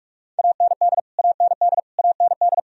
Morse code, heard as a keyed tone, about 46 wpm, 700 Hz. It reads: AND AND AND
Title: Top 100 Words Repeated 45wpm